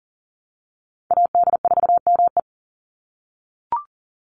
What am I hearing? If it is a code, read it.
Morse code, heard as a keyed tone, about 40 wpm, 700 Hz. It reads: AD4ME
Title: USA Call Signs 40wpm